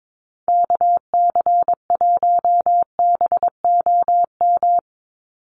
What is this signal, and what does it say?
Morse code, heard as a keyed tone, about 22 wpm, 700 Hz. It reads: KC1BOM
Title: USA Call Signs 22wpm